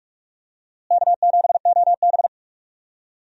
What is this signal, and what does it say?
Morse code, heard as a keyed tone, about 46 wpm, 700 Hz. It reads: K7OB
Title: USA Call Signs - 4 Characters - Repeated 2x - 45wpm